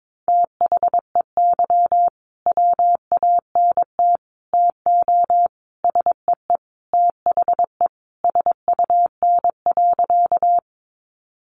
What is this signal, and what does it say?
Morse code, heard as a keyed tone, about 22 wpm, 700 Hz. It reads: THEY WANT TO SEE THE SUN.
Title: Easy Phrases 22wpm